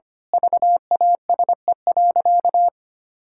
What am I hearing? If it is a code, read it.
Morse code, heard as a keyed tone, about 25 wpm, 700 Hz. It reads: VASE.
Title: Pangram Challenge v2 - 25wpm